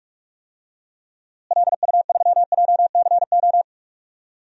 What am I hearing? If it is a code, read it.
Morse code, heard as a keyed tone, about 41 wpm, 700 Hz. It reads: LU3JCO
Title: DXCC-100 Call Signs - 6 Characters - Repeated 3x - 45wpm